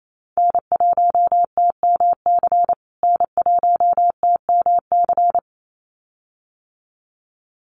Morse code, heard as a keyed tone, about 28 wpm, 700 Hz. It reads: N1TMC N1TMC
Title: USA Call Signs Repeated 2x 28wpm